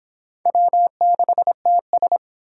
Morse code, heard as a keyed tone, about 26 wpm, 700 Hz. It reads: W6TS